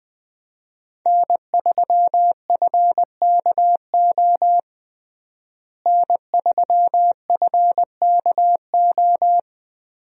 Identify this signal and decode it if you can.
Morse code, heard as a keyed tone, about 20 wpm, 700 Hz. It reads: N3FKO N3FKO
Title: USA Call Signs - 5 Characters - Repeated 2x - 20wpm